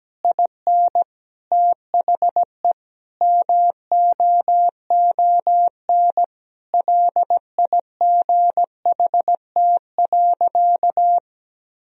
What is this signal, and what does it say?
Morse code, heard as a keyed tone, about 17 wpm, 700 Hz. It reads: IN THE MOON LIGHT.
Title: Sentences from Top 1000 Words - Review Part 4 - 17wpm